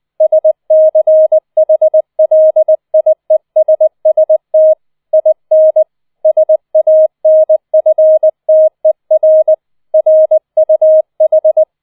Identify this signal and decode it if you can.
Morse code, heard as a keyed tone, about 20 wpm, 630 Hz. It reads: SCHLIESST IN SANFTER RUH